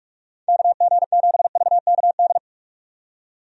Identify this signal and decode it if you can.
Morse code, heard as a keyed tone, about 44 wpm, 700 Hz. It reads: KG7VKD
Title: USA Call Signs - 6 Characters - 45wpm